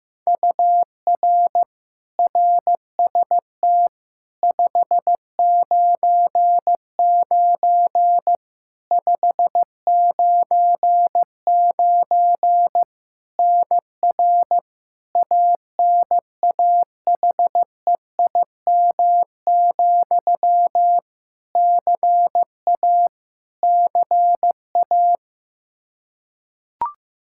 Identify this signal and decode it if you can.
Morse code, heard as a keyed tone, about 15 wpm, 700 Hz. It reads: UR RST 599 599 NR ANAHEIM, CA CA
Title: Practice QSO - Exchange 1 - Beginning - 15wpm